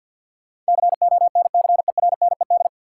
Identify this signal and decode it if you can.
Morse code, heard as a keyed tone, about 52 wpm, 700 Hz. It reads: CONCERNED
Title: Sentences from Top 1000 Words - Review Part 6 - New Word Drill - 50wpm